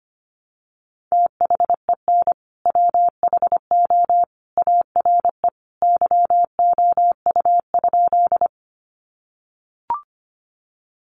Morse code, heard as a keyed tone, about 25 wpm, 700 Hz. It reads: THEN WHO ARE YOU?
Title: Sentences from Top 100 Words 25wpm